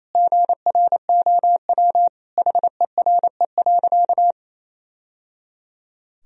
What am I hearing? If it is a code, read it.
Morse code, heard as a keyed tone, about 28 wpm, 700 Hz. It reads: GROW HERE.
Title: Sentences from Top 1000 Words - Review Part 1 - 28wpm